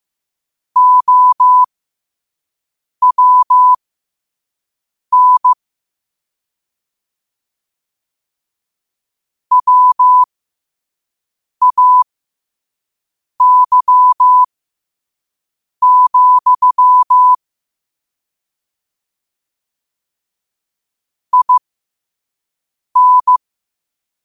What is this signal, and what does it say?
Morse code, heard as a keyed tone, about 15 wpm, spaced as for 5 wpm, 1000 Hz. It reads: OWN WAY, IN
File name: qotdfiles / 5wpm